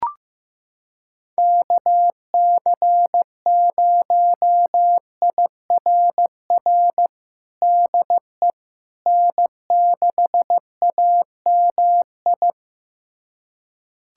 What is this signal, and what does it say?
Morse code, heard as a keyed tone, about 15 wpm, 700 Hz. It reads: KC0IRR DE N6AMI
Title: USA Callsign DE Callsign 15wpm